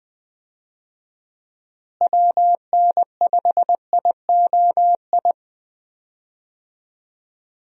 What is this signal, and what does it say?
Morse code, heard as a keyed tone, about 20 wpm, 700 Hz. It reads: WN5IOI
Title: USA Call Signs - 6 Characters - Repeated 2x - 20wpm